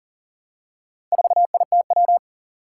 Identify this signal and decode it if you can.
Morse code, heard as a keyed tone, about 40 wpm, 700 Hz. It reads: 4ITW
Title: Four Challenging Letter-Number Combinations 40wpm